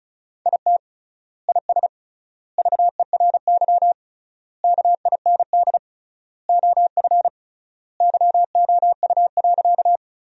Morse code, heard as a keyed tone, about 34 wpm, 700 Hz. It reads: IT IS VERY KIND OF YOU.
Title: 3x Word-Spacing - Sentences from Top 200 Words - 35wpm